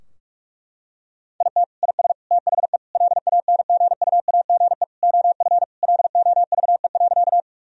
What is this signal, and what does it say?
Morse code, heard as a keyed tone, about 46 wpm, 700 Hz. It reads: IT IS THE LANGUAGE OF LOVE.
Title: Sentences from Top 1000 Words - Review Part 3 - 45wpm